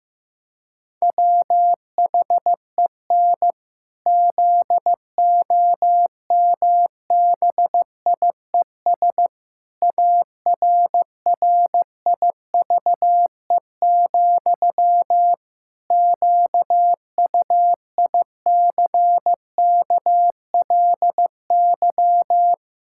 Morse code, heard as a keyed tone, about 16 wpm, 700 Hz. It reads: WHEN ZOMBIES ARRIVE, QUICKLY
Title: Pangram Challenge v2 - 15wpm